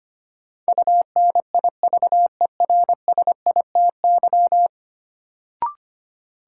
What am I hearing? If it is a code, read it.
Morse code, heard as a keyed tone, about 25 wpm, 700 Hz. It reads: UNIVERSITY
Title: Sentences from Top 1000 Words - Review Part 4 - New Word Drill - 25wpm